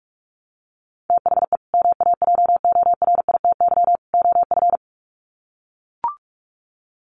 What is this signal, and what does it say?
Morse code, heard as a keyed tone, about 44 wpm, 700 Hz. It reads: THE MAJORITY OF
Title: Sets of 3 Words 45wpm